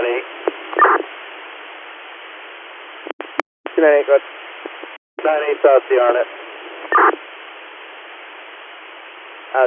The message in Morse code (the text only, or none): none